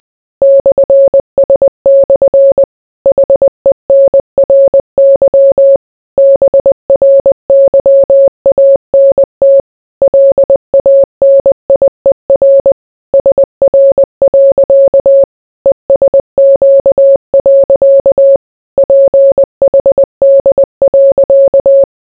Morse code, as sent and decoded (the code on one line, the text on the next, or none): -..-. ... -..-. / .... -. .-. -.-- / -... .-. -.-- .- -. - / .-.. .- -. .. .-. / ... .-. .-.-.- / . ... --.- .-.-.- / .--. .... -.. .-.-.-
/S/ HENRY BRYANT LANIER SR. ESQ. PHD.